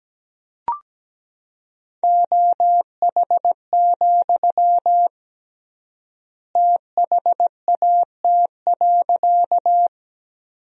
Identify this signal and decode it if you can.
Morse code, heard as a keyed tone, about 17 wpm, 700 Hz. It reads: OH, THAT.
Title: 3x Word-Spacing - Sentences from Top 500 Words - 17wpm